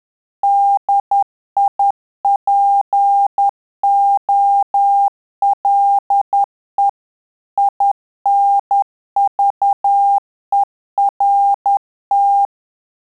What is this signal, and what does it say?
Morse code, heard as a keyed tone, about 10 wpm, 800 Hz. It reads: DIPOLE INVERT
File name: Telegraphy-group / QSO CW 50 CPM